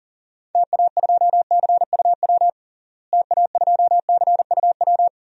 Morse code, heard as a keyed tone, about 40 wpm, 700 Hz. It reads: TA2CUW TA2CUW
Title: DXCC-100 Call Signs - 6 Characters - Repeated 2x - 40wpm